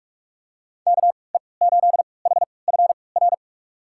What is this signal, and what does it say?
Morse code, heard as a keyed tone, about 46 wpm, spaced as for 25 wpm, 700 Hz. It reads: KE8HFR
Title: USA Call Signs - 6 Characters - 3x Character Spacing - 45wpm